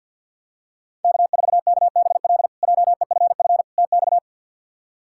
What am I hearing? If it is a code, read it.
Morse code, heard as a keyed tone, about 50 wpm, 700 Hz. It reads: K4KBL JEFF TX
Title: K1USN SST QSO Practice - 50wpm